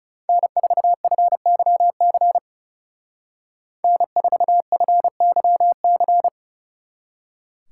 Morse code, heard as a keyed tone, about 36 wpm, 700 Hz. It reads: N4FYC N4FYC
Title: Speed Racing - USA 1x3 Call Signs 30wpm